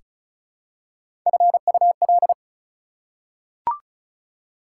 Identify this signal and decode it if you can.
Morse code, heard as a keyed tone, about 36 wpm, 690 Hz. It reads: FUL
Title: Prefixes and Suffixes 35wpm